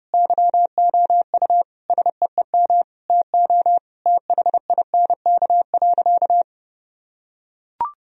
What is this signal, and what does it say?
Morse code, heard as a keyed tone, about 30 wpm, 700 Hz. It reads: YOU SEEM TO THINK.
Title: Sentences from Top 1000 Words - Review Part 1 - 30wpm